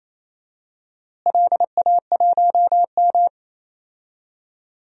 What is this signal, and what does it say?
Morse code, heard as a keyed tone, about 28 wpm, 700 Hz. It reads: LA1M
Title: DXCC-100 Call Signs - 4 Characters - Repeated 3x - 28wpm